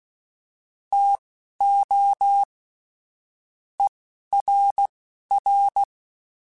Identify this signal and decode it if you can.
Morse code, heard as a keyed tone, about 16 wpm, spaced as for 12 wpm, 800 Hz. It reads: TO ERR
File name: Chuckles / Chuckles-mp3s / HumanErr @10